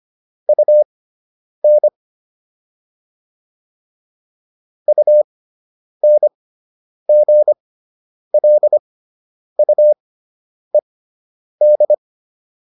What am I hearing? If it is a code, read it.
Morse code, heard as a keyed tone, about 25 wpm, spaced as for 9 wpm, 600 Hz. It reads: UN UNGLUED